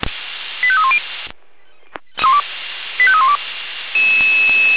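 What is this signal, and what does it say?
Five-tone selective call (ZVEI), heard as quick runs of five tones, tones of about 70 ms. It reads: 85310 85312
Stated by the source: ZVEI five-tone selective call 85312, received by SDR.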